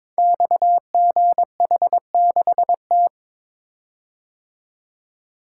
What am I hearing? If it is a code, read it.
Morse code, heard as a keyed tone, about 22 wpm, 700 Hz. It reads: XGH6T
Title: Code Groups - 5 Letter:Numbers - 22wpm